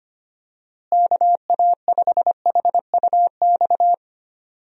Morse code, heard as a keyed tone, about 25 wpm, 700 Hz. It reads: KA5HUX
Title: USA Call Signs - 6 Characters - Repeated 2x - 25wpm